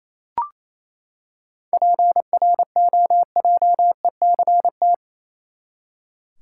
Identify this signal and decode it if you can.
Morse code, heard as a keyed tone, about 28 wpm, 700 Hz. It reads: PROJECT
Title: Seven Letter Words - 28wpm